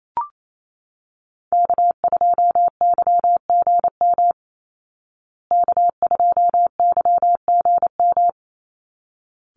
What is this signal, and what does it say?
Morse code, heard as a keyed tone, about 28 wpm, 690 Hz. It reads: K2YGM K2YGM
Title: Call Signs Repeated 3x - 28wpm